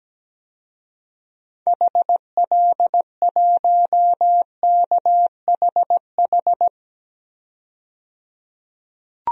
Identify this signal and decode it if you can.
Morse code, heard as a keyed tone, about 17 wpm, 700 Hz. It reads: HL1KHH
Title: DXCC-100 Call Signs - 6 Characters - 17wpm